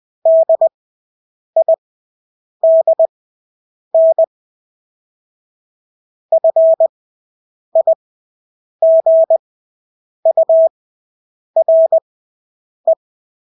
Morse code, heard as a keyed tone, about 20 wpm, spaced as for 8 wpm, 650 Hz. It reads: DIDN FIGURE
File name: Engelske ord 20/8 wpm